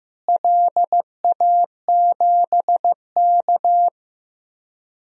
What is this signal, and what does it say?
Morse code, heard as a keyed tone, about 15 wpm, 700 Hz. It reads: LA7K
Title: DXCC-100 Call Signs - 4 Characters - Repeated 2x - 15wpm